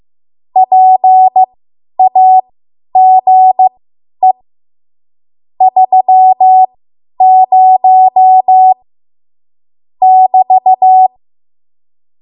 Morse code, heard as a keyed tone, about 15 wpm, spaced as for 10 wpm, 750 Hz. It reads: PAGE 30 <BT>
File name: Morse / 10WPM